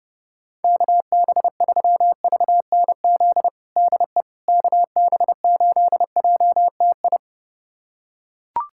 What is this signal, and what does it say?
Morse code, heard as a keyed tone, about 32 wpm, 700 Hz. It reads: KB3VNZ DE KB8JTI
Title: USA Callsign DE Callsign 30wpm